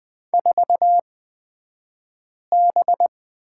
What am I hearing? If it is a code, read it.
Morse code, heard as a keyed tone, about 20 wpm, spaced as for 5 wpm, 700 Hz. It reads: 4B
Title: 249 Lesson 40d - BK - Two Characters 20wpm